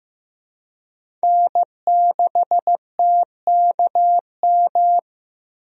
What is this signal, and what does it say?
Morse code, heard as a keyed tone, about 15 wpm, 700 Hz. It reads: N6TKM
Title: USA Call Signs - 5 Characters - Repeated 2x - 15wpm